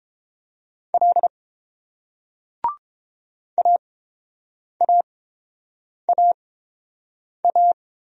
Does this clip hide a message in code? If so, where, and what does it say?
Yes: Morse code from 0.9 to 7.7 s, heard as a keyed tone, about 30 wpm, spaced as for 7 wpm, 700 Hz. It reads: L AAAA